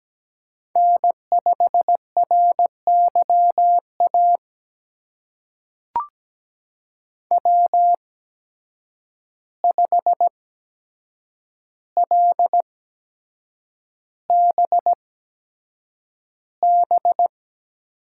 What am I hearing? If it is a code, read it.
Morse code, heard as a keyed tone, about 17 wpm, spaced as for 7 wpm, 700 Hz. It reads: N5RYA W5LBB
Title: USA Call Signs - 5 Characters - 8x Character Spacing - 17wpm